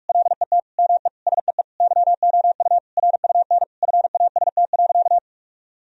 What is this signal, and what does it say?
Morse code, heard as a keyed tone, about 43 wpm, 700 Hz. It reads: LET ME SEE YOU RUN FAST.
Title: Sentences from Top 1000 Words - Review Part 2 - 45wpm